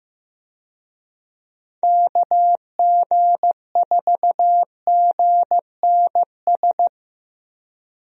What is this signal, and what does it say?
Morse code, heard as a keyed tone, about 15 wpm, 700 Hz. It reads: KG4GNS
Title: USA Call Signs - 6 Characters - Repeated 2x - 15wpm